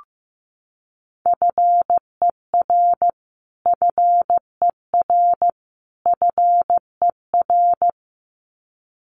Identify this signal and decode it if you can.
Morse code, heard as a keyed tone, about 15 wpm, 700 Hz. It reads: FER FER FER
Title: Abbreviations Repeated 3x - 15wpm